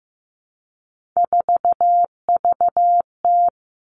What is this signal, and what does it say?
Morse code, heard as a keyed tone, about 15 wpm, 700 Hz. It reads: 4VT
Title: All Cut Numbers - 3-Digit 15wpm